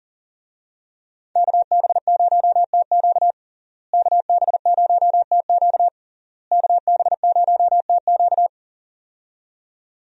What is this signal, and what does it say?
Morse code, heard as a keyed tone, about 40 wpm, 700 Hz. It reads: KB0TQ KB0TQ KB0TQ
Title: USA Call Signs - 5 Characters - Repeated 3x - 40wpm